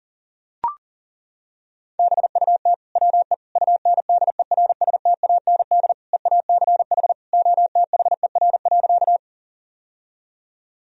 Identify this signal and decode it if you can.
Morse code, heard as a keyed tone, about 37 wpm, 700 Hz. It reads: BUT WE UNDERSTAND EACH OTHER.
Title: Sentences from Top 1000 Words - Review Part 1 - 40wpm